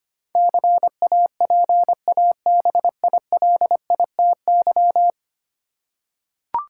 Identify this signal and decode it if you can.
Morse code, heard as a keyed tone, about 25 wpm, 700 Hz. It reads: CAPABILITY
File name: 10 Letter Words - First 3 Letters Given - 25wpm